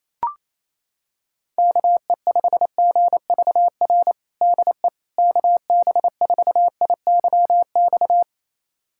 Morse code, heard as a keyed tone, about 28 wpm, 700 Hz. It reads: KE5GVR DE KB4IYX
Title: USA Callsign DE Callsign 28wpm